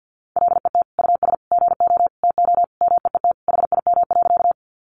Morse code, heard as a keyed tone, about 50 wpm, 700 Hz. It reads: LET US GO TO MEET HIM.
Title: Sentences from Top 1100 Words 50wpm